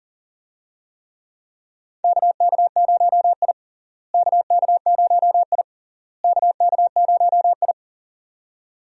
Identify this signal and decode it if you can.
Morse code, heard as a keyed tone, about 40 wpm, 700 Hz. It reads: KK0I KK0I KK0I
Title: USA Call Signs - 4 Characters - Repeated 3x - 40wpm